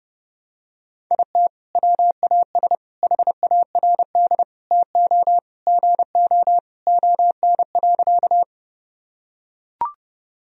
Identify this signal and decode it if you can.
Morse code, heard as a keyed tone, about 30 wpm, 700 Hz. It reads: IT WAS HARD TO GO ON.
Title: Sentences from Top 1000 Words - Review Part 1 - 30wpm